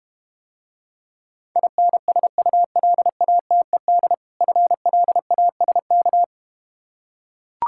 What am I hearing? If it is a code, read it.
Morse code, heard as a keyed tone, about 33 wpm, 700 Hz. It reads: INSULATED FLASK